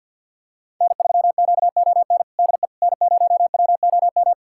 Morse code, heard as a keyed tone, about 52 wpm, 700 Hz. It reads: N3QON DE N0WOM
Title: USA Callsign DE Callsign - 5 Characters - 50wpm